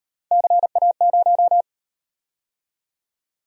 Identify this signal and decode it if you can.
Morse code, heard as a keyed tone, about 34 wpm, 700 Hz. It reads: CA0